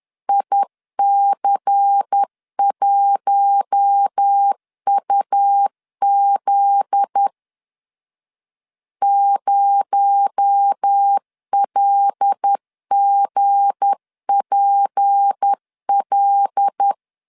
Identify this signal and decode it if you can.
Morse code, heard as a keyed tone, about 10 wpm, 800 Hz. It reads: IC1UZ 0LGPL
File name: LEZIONI / 7 GF9Z / 7 GF9Z 10wpm 2611